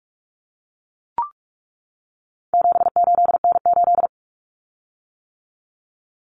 Morse code, heard as a keyed tone, about 44 wpm, 700 Hz. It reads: 78N8
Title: Common Cut Numbers - 4-Digit 45wpm